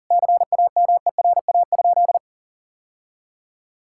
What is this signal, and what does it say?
Morse code, heard as a keyed tone, about 40 wpm, 690 Hz. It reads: CAMERA?
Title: Sentences from Top 1000 Words - Review Part 5 - 40wpm